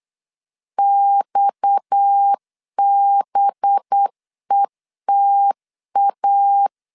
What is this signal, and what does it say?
Morse code, heard as a keyed tone, about 8 wpm, 800 Hz. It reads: XBETA
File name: CW / DET / 1 9 1500C S B 8wpm 5624